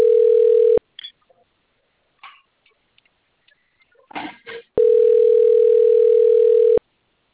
Ringback tone (North America), 440 + 480 Hz.